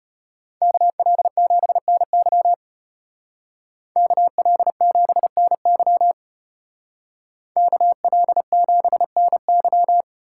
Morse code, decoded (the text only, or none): KL7NY KL7NY KL7NY